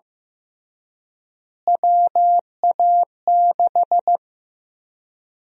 Morse code, heard as a keyed tone, about 15 wpm, 700 Hz. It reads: WA6
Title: USA Call Sign Prefixes - Repeated 2x - 15wpm